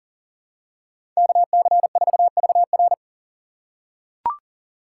Morse code, heard as a keyed tone, about 40 wpm, 700 Hz. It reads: KC4VR
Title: USA Call Signs - 5 Characters - 2x Character Spacing - 40wpm